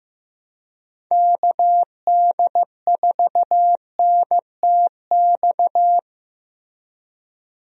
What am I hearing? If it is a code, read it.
Morse code, heard as a keyed tone, about 15 wpm, 700 Hz. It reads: KD4NTX
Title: USA Call Signs - 6 Characters - Repeated 2x - 15wpm